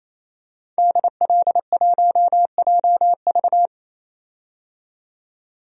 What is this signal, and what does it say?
Morse code, heard as a keyed tone, about 28 wpm, 700 Hz. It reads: DL1JV
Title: DXCC-100 Call Signs - 5 Characters - 28wpm